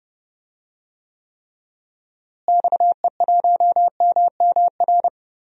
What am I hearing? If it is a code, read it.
Morse code, heard as a keyed tone, about 30 wpm, 700 Hz. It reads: XE1MMR
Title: DXCC-100 Call Signs - 6 Characters - Repeated 3x - 30wpm